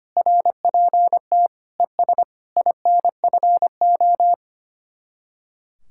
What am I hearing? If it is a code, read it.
Morse code, heard as a keyed tone, about 24 wpm, 700 Hz. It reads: RPT ES INFO
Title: Speed-Racing Common QSO Phrases - 25wpm